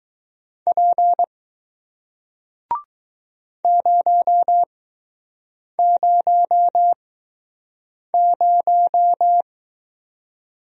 Morse code, heard as a keyed tone, about 20 wpm, spaced as for 6 wpm, 700 Hz. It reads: P 000